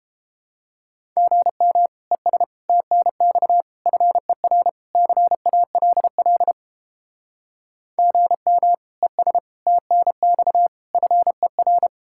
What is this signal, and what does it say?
Morse code, heard as a keyed tone, about 29 wpm, 700 Hz. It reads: GM ES TNX FER CALL GM ES TNX FER